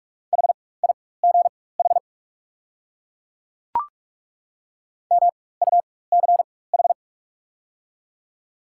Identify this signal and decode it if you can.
Morse code, heard as a keyed tone, about 43 wpm, spaced as for 21 wpm, 690 Hz. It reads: HIGH MUCH
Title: 4x Character Spacing - Top 300 Words - 45wpm